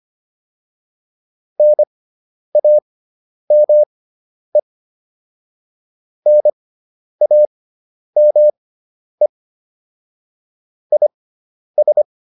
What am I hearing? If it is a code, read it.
Morse code, heard as a keyed tone, about 25 wpm, spaced as for 10 wpm, 600 Hz. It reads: NAME NAME IS
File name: QSO402 10